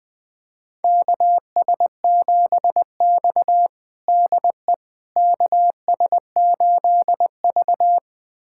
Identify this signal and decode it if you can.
Morse code, heard as a keyed tone, about 20 wpm, 700 Hz. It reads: KS7X DE KS8V
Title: USA Callsign DE Callsign - 4 Characters - 20wpm